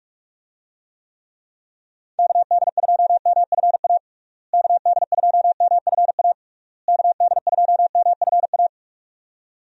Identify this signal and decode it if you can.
Morse code, heard as a keyed tone, about 46 wpm, 700 Hz. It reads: KD2MFA KD2MFA KD2MFA
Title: USA Call Signs - 6 Characters - Repeated 3x - 45wpm